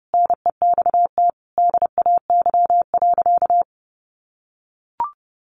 Morse code, heard as a keyed tone, about 30 wpm, 700 Hz. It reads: NEXT DAY.
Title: Sentences from Top 1400 Words 30wpm